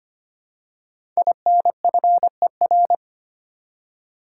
Morse code, heard as a keyed tone, about 25 wpm, 700 Hz. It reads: INFER
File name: 5 Letter Words - First 3 Letters Given - 25wpm